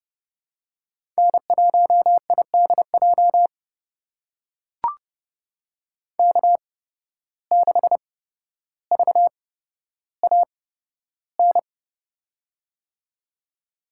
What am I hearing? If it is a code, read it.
Morse code, heard as a keyed tone, about 30 wpm, spaced as for 13 wpm, 700 Hz. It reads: N1IDJ K6VAN
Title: USA Call Signs - 5 Characters - 8x Character Spacing - 30wpm